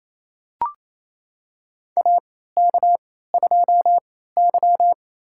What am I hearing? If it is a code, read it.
Morse code, heard as a keyed tone, about 28 wpm, spaced as for 16 wpm, 700 Hz. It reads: AK2Y
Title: USA Call Signs - 4 Characters - 3x Character Spacing - 28wpm